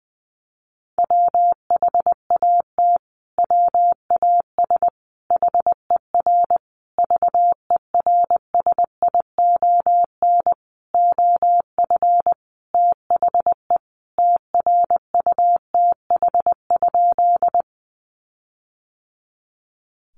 Morse code, heard as a keyed tone, about 20 wpm, 700 Hz. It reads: WHAT WAS HER VERSION OF THE TRUTH?
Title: Sentences from Top 1400 Words 20wpm